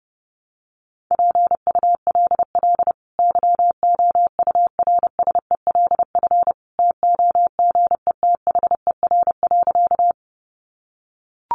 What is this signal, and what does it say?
Morse code, heard as a keyed tone, about 28 wpm, 700 Hz. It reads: PULL YOURSELF TOGETHER.
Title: Sentences from Top 600 Words 30wpm